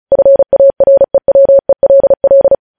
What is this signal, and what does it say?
Morse code, heard as a keyed tone, about 36 wpm, 550 Hz. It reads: FAREWELL
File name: podcast / News headlines-1x-35x35